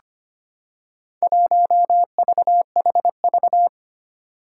Morse code, heard as a keyed tone, about 25 wpm, 700 Hz. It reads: 1VHV
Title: Four Challenging Letter-Number Combinations 25wpm